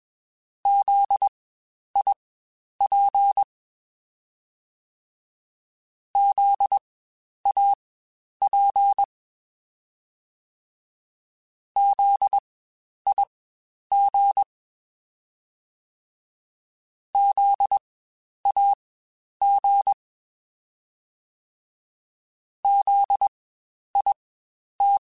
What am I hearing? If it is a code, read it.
Morse code, heard as a keyed tone, about 21 wpm, spaced as for 10 wpm, 800 Hz. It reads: ZIP ZAP ZIG ZAG ZIT
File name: Spaced Alphas / Spaced Alphas mp3s / Zs sp @20